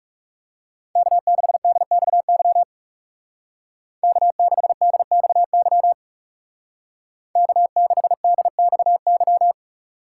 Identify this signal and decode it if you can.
Morse code, heard as a keyed tone, about 37 wpm, 700 Hz. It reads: K6DXY K6DXY K6DXY
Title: Speed Racing - USA 1x3 Call Signs 30wpm